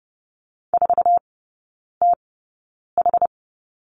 Morse code, heard as a keyed tone, about 30 wpm, spaced as for 9 wpm, 700 Hz. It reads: 4TH
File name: course / 30:10wpm / 087 Lesson 15e - Number 5 - Three Characters 30wpm